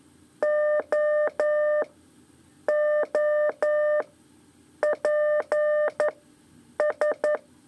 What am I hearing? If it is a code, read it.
Morse code, heard as a keyed tone, about 11 wpm, 600 Hz. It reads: O O P S
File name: 3 lezione CW 2 AIR